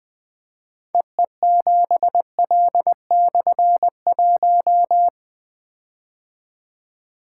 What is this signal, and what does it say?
Morse code, heard as a keyed tone, about 20 wpm, 700 Hz. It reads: EE7L/1
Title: Difficult Call Signs 20wpm